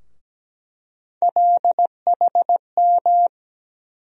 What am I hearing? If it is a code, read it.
Morse code, heard as a keyed tone, about 17 wpm, 700 Hz. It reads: LHM
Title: USA Call Sign Postfixes - 3 Letters 17wpm